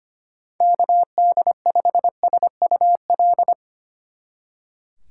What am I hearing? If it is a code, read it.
Morse code, heard as a keyed tone, about 31 wpm, 700 Hz. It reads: KD5SUL